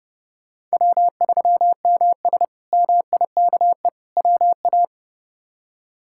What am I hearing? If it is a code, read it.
Morse code, heard as a keyed tone, about 30 wpm, 700 Hz. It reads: W3MS MIKE WA
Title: K1USN SST QSO Practice - 30wpm